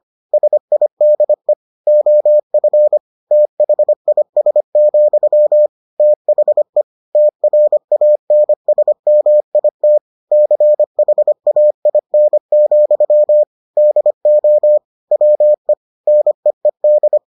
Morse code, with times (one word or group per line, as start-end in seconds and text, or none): 0.3-1.5 SIDE
1.9-3.0 OF
3.3-5.7 THIS,
6.0-6.8 THE
7.1-10.0 TRANSMIT
10.3-13.4 CHAIN,
13.8-14.8 DO
15.1-15.7 WE
16.1-17.2 NEED